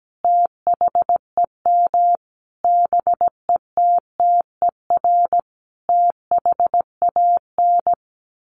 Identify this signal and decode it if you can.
Morse code, heard as a keyed tone, about 17 wpm, 700 Hz. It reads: THEM BETTER THAN
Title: Sentences from Top 1200 Words 17wpm